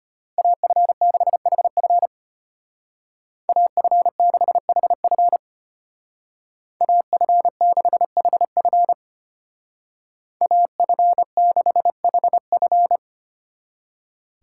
Morse code, heard as a keyed tone, about 30 wpm, 700 Hz. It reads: AF6HF AF6HF AF6HF AF6HF
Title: Speed Racing - USA 2x2 Call Signs 25wpm